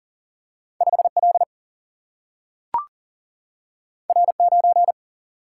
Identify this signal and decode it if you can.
Morse code, heard as a keyed tone, about 40 wpm, 700 Hz. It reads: 5 L R 9
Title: Single-Letter Single-Number 40wpm